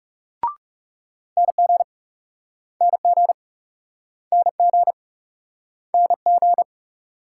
Morse code, heard as a keyed tone, about 38 wpm, 700 Hz. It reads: NG NG NG NG